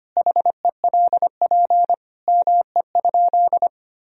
Morse code, heard as a keyed tone, about 25 wpm, 700 Hz. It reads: HELP ME?
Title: Sentences from Top 1000 Words - Review Part 3 - 25wpm